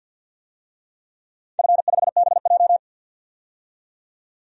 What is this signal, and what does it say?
Morse code, heard as a keyed tone, about 47 wpm, 700 Hz. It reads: F5BJ
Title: DXCC-100 Call Signs - 4 Characters - 50wpm